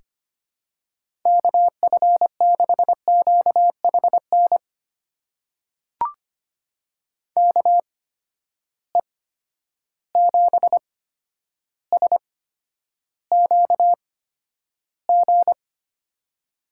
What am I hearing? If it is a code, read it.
Morse code, heard as a keyed tone, about 25 wpm, spaced as for 11 wpm, 700 Hz. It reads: KF6QHN KE7SQG
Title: USA Call Signs - 6 Characters - 8x Character Spacing - 25wpm